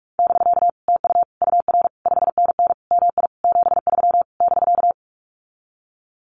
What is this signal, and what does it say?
Morse code, heard as a keyed tone, about 46 wpm, 700 Hz. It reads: <BK> TU UR 5NN MI 73 <BK>
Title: Common QSO Phrases Encore 45wpm